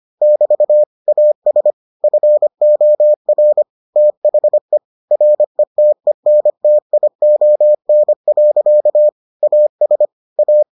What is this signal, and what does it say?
Morse code, heard as a keyed tone, about 25 wpm, 600 Hz. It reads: <BT> AS FOR THE RETENTION. AS A